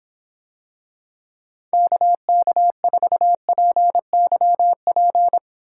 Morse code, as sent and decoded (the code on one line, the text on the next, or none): -.- -.- ....- .--. -.-- .--.
KK4PYP